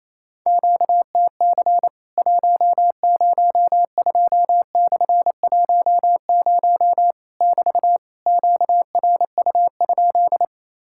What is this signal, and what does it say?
Morse code, heard as a keyed tone, about 28 wpm, 700 Hz. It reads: QTC 102/10 <BT> QRU?
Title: Worked All Europe DX Contest - QTC Exchange - 28wpm